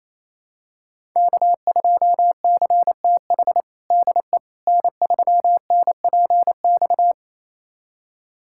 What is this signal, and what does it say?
Morse code, heard as a keyed tone, about 28 wpm, 700 Hz. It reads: K2CTH DE N3NPX
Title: USA Callsign DE Callsign - 5 Characters - 28wpm